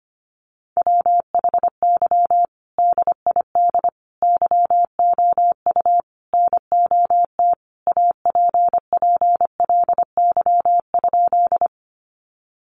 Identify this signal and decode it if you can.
Morse code, heard as a keyed tone, about 25 wpm, 700 Hz. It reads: WHY DID YOU NOT APPLY?
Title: Sentences from Top 1200 Words 25wpm